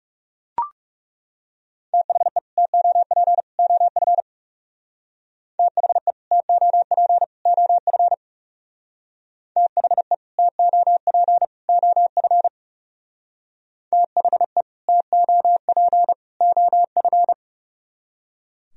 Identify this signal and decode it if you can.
Morse code, heard as a keyed tone, about 38 wpm, 700 Hz. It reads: THE TOP OF THE TOP OF THE TOP OF THE TOP OF